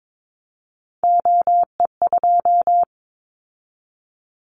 Morse code, heard as a keyed tone, about 22 wpm, 700 Hz. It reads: OE2
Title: Callsign Prefixes for Top 100 DXCC - 22wpm